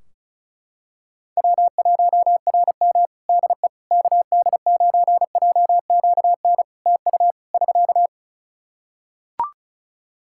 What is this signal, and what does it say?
Morse code, heard as a keyed tone, about 36 wpm, 700 Hz. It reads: W1RM DE KD9JQN TU <SK>
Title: K1USN SST QSO Practice 35wpm